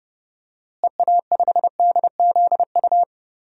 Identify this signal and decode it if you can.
Morse code, heard as a keyed tone, about 32 wpm, 700 Hz. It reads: EA5DZU
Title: DXCC-100 Call Signs - 6 Characters - Repeated 3x - 30wpm